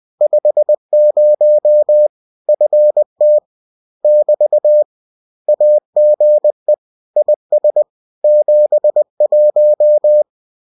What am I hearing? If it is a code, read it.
Morse code, heard as a keyed tone, about 20 wpm, 600 Hz. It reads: 50 FT <BT> AGE IS 71